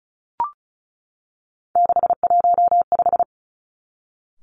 Morse code, heard as a keyed tone, about 36 wpm, 700 Hz. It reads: 615
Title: Three Challenging Letter-Number Combinations 35wpm